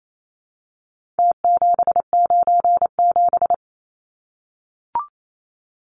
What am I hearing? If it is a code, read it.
Morse code, heard as a keyed tone, about 28 wpm, 700 Hz. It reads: T797
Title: Numbers - Contest Style - 4-Digit 28wpm